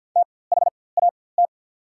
Morse code, heard as a keyed tone, about 48 wpm, spaced as for 23 wpm, 690 Hz. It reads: THAT
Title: 4x Character Spacing - Top 100 Words - 50wpm